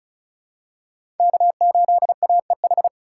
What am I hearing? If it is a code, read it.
Morse code, heard as a keyed tone, about 36 wpm, 700 Hz. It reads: K8AEH
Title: USA Call Signs - 5 Characters - 6x Character Spacing - 35wpm